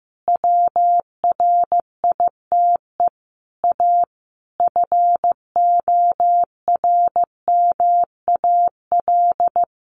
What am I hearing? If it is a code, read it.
Morse code, heard as a keyed tone, about 15 wpm, 700 Hz. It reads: WRITE A FORMAL
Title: Sentences from Top 1300 Words 15wpm